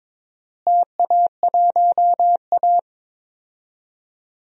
Morse code, heard as a keyed tone, about 22 wpm, 700 Hz. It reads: TA1A
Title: DXCC-100 Call Signs - 4 Characters - Repeated 2x - 22wpm